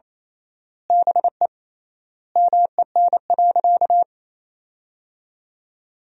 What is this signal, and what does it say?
Morse code, heard as a keyed tone, about 28 wpm, 700 Hz. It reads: BE MEN.
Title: 3x Word-Spacing - Sentences from Top 300 Words - 28wpm